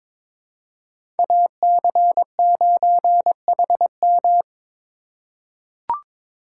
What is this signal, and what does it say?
Morse code, heard as a keyed tone, about 22 wpm, 700 Hz. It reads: AC9HM
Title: USA Call Signs - 5 Characters - 4x Character Spacing - 22wpm